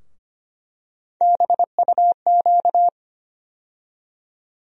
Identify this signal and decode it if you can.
Morse code, heard as a keyed tone, about 25 wpm, 700 Hz. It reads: BUQ
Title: USA Call Sign Postfixes - 3 Letters 25wpm